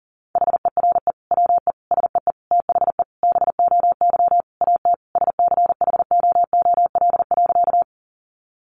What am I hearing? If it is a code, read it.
Morse code, heard as a keyed tone, about 40 wpm, 700 Hz. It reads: HERE WE SEE THE BOY AT SCHOOL.
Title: Sentences from Top 300 Words 40wpm